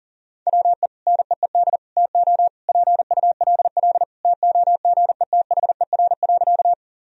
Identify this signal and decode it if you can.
Morse code, heard as a keyed tone, about 37 wpm, 700 Hz. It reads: WE NEED TO PULL TOGETHER.
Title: Sentences from Top 1000 Words - Review Part 2 - 40wpm